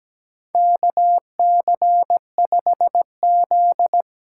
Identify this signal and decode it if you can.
Morse code, heard as a keyed tone, about 17 wpm, 700 Hz. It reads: KC5Z
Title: USA Call Signs - 5 Characters - 8x Character Spacing - 17wpm